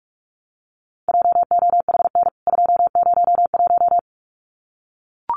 Morse code, heard as a keyed tone, about 44 wpm, 700 Hz. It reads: JOHN 201
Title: CWT Exchange 45wpm